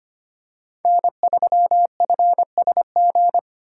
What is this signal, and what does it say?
Morse code, heard as a keyed tone, about 25 wpm, 700 Hz. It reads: N3FSG
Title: USA Call Signs - 5 Characters - 4x Character Spacing - 25wpm